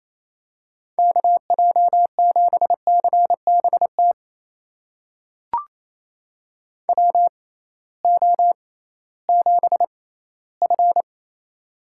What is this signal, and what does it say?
Morse code, heard as a keyed tone, about 28 wpm, spaced as for 16 wpm, 700 Hz. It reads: KJ7CBT WO7F
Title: USA Call Signs - 6x Character Spacing - 28wpm